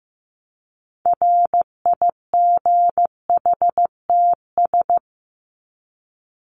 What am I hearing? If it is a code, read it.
Morse code, heard as a keyed tone, about 15 wpm, 700 Hz. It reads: RIGHTS
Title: ES:S Suffix - Top 100 Words with Base Words 15wpm